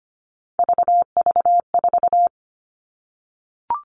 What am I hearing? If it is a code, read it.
Morse code, heard as a keyed tone, about 25 wpm, 700 Hz. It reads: VV4
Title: All Cut Numbers - 3-Digit 25wpm